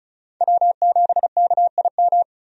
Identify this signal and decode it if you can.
Morse code, heard as a keyed tone, about 36 wpm, 700 Hz. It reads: W7KIM
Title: USA Call Signs - 5 Characters - 35wpm